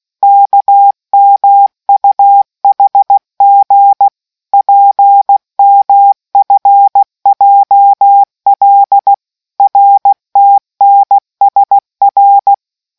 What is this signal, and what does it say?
Morse code, heard as a keyed tone, about 16 wpm, 800 Hz. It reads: KMUHG PMFJL RTNSR